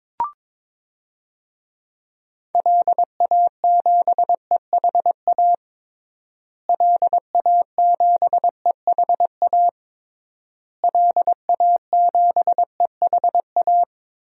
Morse code, heard as a keyed tone, about 22 wpm, 700 Hz. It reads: LA7EHA LA7EHA LA7EHA
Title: DXCC-100 Call Signs - 6 Characters - Repeated 3x - 22wpm